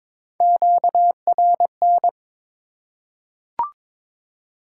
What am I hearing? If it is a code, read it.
Morse code, heard as a keyed tone, about 22 wpm, 700 Hz. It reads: QRN
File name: Top 90 QSO Elements - 3x Character Spacing - 22wpm